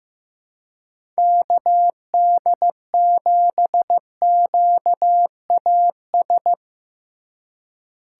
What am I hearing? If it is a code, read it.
Morse code, heard as a keyed tone, about 15 wpm, 700 Hz. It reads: KD7QAS
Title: USA Call Signs - 6 Characters - 4x Character Spacing - 15wpm